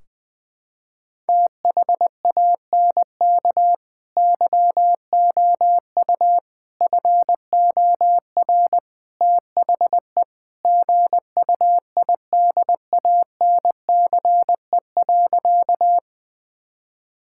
Morse code, heard as a keyed tone, about 21 wpm, 700 Hz. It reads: THANK YOU FOR THE GUIDANCE.
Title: Sentences from Top 1600 Words 20wpm